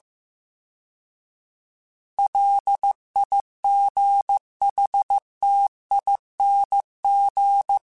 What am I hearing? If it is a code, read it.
Morse code, heard as a keyed tone, about 15 wpm, 800 Hz. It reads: LIGHTING